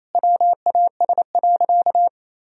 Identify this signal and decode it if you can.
Morse code, heard as a keyed tone, about 28 wpm, 700 Hz. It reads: WAS.
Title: Sentences from Top 1000 Words - Review Part 2 - 28wpm